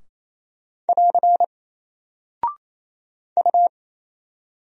Morse code, heard as a keyed tone, about 27 wpm, 700 Hz. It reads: <AR> U
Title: Single Character (Letter:Number:Punctuation:Prosign) 28wpm